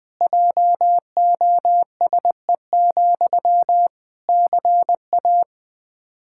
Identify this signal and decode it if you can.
Morse code, heard as a keyed tone, about 20 wpm, 700 Hz. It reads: JOSE, CA
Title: Practice QSO - Exchange 1 - Beginning - 20wpm